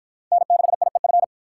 Morse code, heard as a keyed tone, about 52 wpm, 700 Hz. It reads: N6IEF